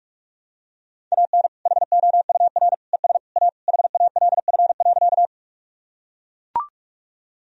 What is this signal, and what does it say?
Morse code, heard as a keyed tone, about 42 wpm, 700 Hz. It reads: AN HOUR ES A HALF.
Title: Sentences from Top 400 Words and Abbreviations 45wpm